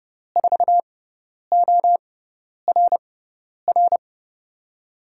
Morse code, heard as a keyed tone, about 30 wpm, spaced as for 10 wpm, 700 Hz. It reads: 4ORR
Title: USA Call Signs - 6 Characters - 6x Character Spacing - 30wpm